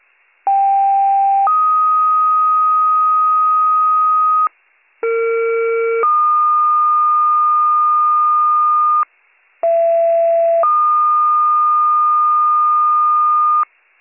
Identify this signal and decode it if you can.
Two-tone page, heard as a one-second tone, then a three-second tone, about 790 then 1250 Hz, about 460 then 1220 Hz, about 670 then 1220 Hz. It is heard from 0.5 to 13.6 s.